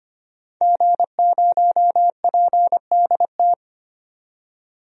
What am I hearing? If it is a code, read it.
Morse code, heard as a keyed tone, about 25 wpm, 700 Hz. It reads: G0PDT
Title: DXCC-100 Call Signs - 5 Characters - 25wpm